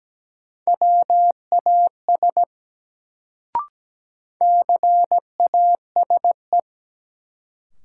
Morse code, heard as a keyed tone, about 17 wpm, 700 Hz. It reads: WAS CASE
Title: Sets of 2 Words - New Word Drill - 17wpm